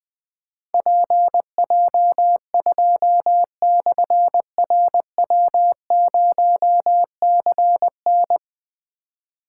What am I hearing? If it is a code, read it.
Morse code, heard as a keyed tone, about 20 wpm, 700 Hz. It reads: PJ2/RW0CN
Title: Difficult Call Signs 20wpm